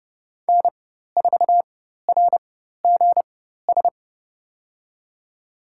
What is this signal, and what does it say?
Morse code, heard as a keyed tone, about 30 wpm, spaced as for 14 wpm, 700 Hz. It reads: N4RGS
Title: USA Call Signs - 5 Characters - 4x Character Spacing - 30wpm